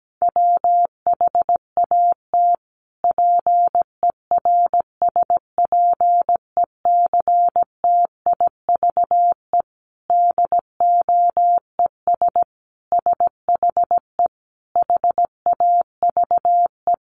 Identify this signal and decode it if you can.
Morse code, heard as a keyed tone, about 17 wpm, 700 Hz. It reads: WHAT PERSPECTIVE DOES SHE HAVE
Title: Sentences from Top 1200 Words 17wpm